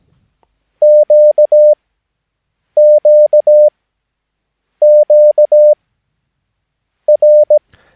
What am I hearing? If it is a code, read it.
Morse code, heard as a keyed tone, about 17 wpm, spaced as for 6 wpm, 600 Hz. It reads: QQQR
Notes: Övningsmaterial med alfabetet A till Ö samt siffror 0 till 9 enligt International/Continental Morse Code.